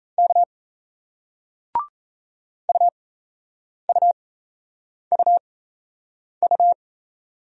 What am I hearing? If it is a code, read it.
Morse code, heard as a keyed tone, about 38 wpm, spaced as for 8 wpm, 700 Hz. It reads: K UUUU